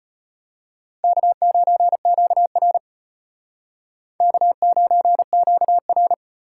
Morse code, heard as a keyed tone, about 32 wpm, 700 Hz. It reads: K9QR K9QR